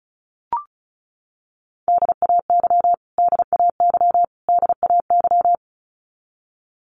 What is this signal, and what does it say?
Morse code, heard as a keyed tone, about 34 wpm, 700 Hz. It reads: DAY DAY DAY
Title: Top 500 Words Repeated 35wpm